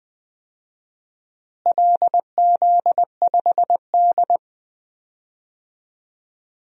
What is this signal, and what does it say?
Morse code, heard as a keyed tone, about 20 wpm, 700 Hz. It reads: LZ5D
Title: DXCC-100 Call Signs - 4 Characters - Repeated 2x - 20wpm